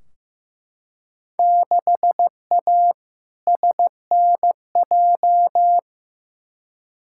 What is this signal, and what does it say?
Morse code, heard as a keyed tone, about 16 wpm, 700 Hz. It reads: 6A SNJ
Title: Field Day Exchanges Encore - 15wpm